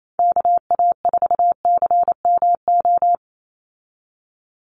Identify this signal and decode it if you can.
Morse code, heard as a keyed tone, about 28 wpm, 700 Hz. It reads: KA4CMO
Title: USA Call Signs 28wpm